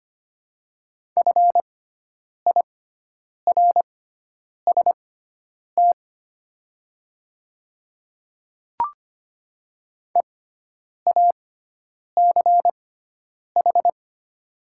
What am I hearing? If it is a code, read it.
Morse code, heard as a keyed tone, about 25 wpm, spaced as for 9 wpm, 700 Hz. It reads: FIRST EACH
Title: 6x Character Spacing - Top 100 Words - 25wpm